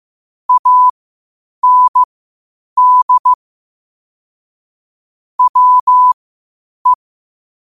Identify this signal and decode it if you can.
Morse code, heard as a keyed tone, about 15 wpm, spaced as for 8 wpm, 1000 Hz. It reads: AND WE
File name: qotdfiles / 8wpm